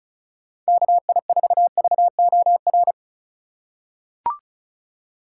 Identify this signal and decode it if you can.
Morse code, heard as a keyed tone, about 36 wpm, 700 Hz. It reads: KI4VOR
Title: USA Call Signs - 3x Character Spacing - 35wpm